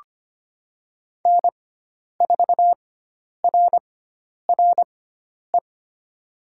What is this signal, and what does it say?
Morse code, heard as a keyed tone, about 25 wpm, spaced as for 10 wpm, 700 Hz. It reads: N4RRE
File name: Lesson 16g - Letter c - Call Signs 25wpm